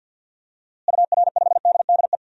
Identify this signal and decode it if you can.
Morse code, heard as a keyed tone, about 47 wpm, 700 Hz. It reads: UR5DDE
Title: DXCC-100 Call Signs - 6 Characters - Repeated 3x - 50wpm